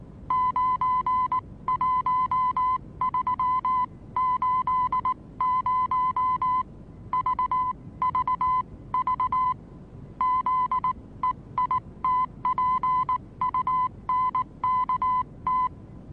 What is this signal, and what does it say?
Morse code, heard as a keyed tone, about 19 wpm, spaced as for 16 wpm, 1010 Hz. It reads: 91380 VVV ZEITPUNKT
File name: repos / Morsekurs / DJ4UF Morsen26